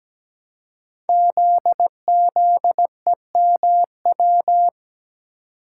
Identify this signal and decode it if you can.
Morse code, heard as a keyed tone, about 17 wpm, 700 Hz. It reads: ZZEMW
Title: Code Groups - 5 Letters - 17wpm